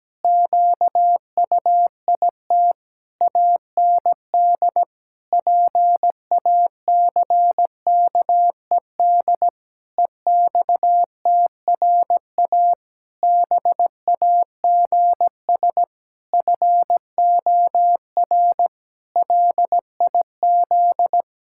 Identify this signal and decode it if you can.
Morse code, heard as a keyed tone, about 17 wpm, 700 Hz. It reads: QUIT AND PACKED EXTRA BAGS FOR LIZ
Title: Pangram Challenge v2 - 17wpm